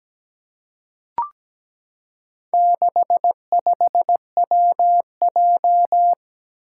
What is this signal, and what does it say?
Morse code, heard as a keyed tone, about 17 wpm, 700 Hz. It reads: 65WJ
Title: Four Challenging Letter-Number Combinations 17wpm